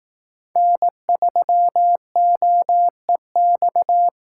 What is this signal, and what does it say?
Morse code, heard as a keyed tone, about 18 wpm, 700 Hz. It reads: N3OEX
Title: Speed Racing - USA Call Signs 15wpm